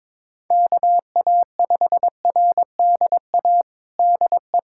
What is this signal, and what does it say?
Morse code, heard as a keyed tone, about 22 wpm, 700 Hz. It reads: KA5RDA DE
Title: USA Callsign DE Callsign 22wpm